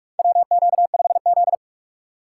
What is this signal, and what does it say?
Morse code, heard as a keyed tone, about 43 wpm, 700 Hz. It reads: WQ5Z
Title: USA Call Signs - 4 Characters - 6x Character Spacing - 45wpm